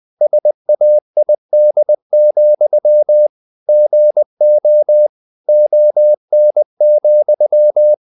Morse code, heard as a keyed tone, about 20 wpm, 600 Hz. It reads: SAID, GO ON,
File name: the old man and the sea part 2 20wpm